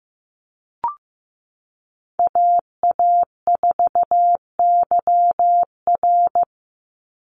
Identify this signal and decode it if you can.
Morse code, heard as a keyed tone, about 14 wpm, 700 Hz. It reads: AA4YR
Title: USA Call Signs 15wpm